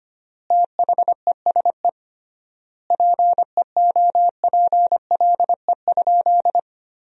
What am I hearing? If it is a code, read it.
Morse code, heard as a keyed tone, about 25 wpm, 700 Hz. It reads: THESE PEOPLE?
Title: 3x Word-Spacing - Sentences from Top 200 Words - 25wpm